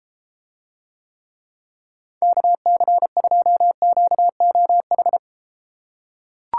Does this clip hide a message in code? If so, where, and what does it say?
Yes: Morse code from 2.2 to 5.2 s, heard as a keyed tone, about 33 wpm, 700 Hz. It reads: KC2QOH